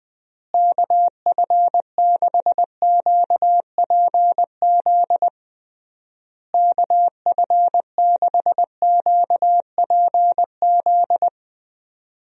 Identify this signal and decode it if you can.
Morse code, heard as a keyed tone, about 19 wpm, 700 Hz. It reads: KF6QPZ KF6QPZ
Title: USA Call Signs - 6 Characters - Repeated 3x - 20wpm